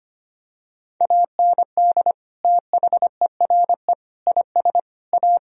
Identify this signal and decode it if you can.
Morse code, heard as a keyed tone, about 25 wpm, 700 Hz. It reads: AND THERE IS A
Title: Sentences from Top 1000 Words - Review Part 1 - 25wpm